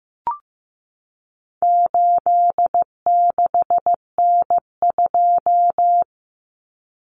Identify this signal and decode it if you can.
Morse code, heard as a keyed tone, about 15 wpm, 700 Hz. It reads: 86N2
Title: Common Cut Numbers - 4-Digit 15wpm